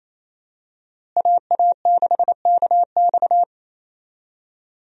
Morse code, heard as a keyed tone, about 28 wpm, 700 Hz. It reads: AA6KX
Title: USA Call Signs - 5 Characters - Repeated 2x - 28wpm